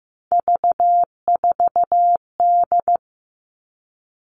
Morse code, heard as a keyed tone, about 15 wpm, 700 Hz. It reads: V4D
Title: All Cut Numbers - 3-Digit 15wpm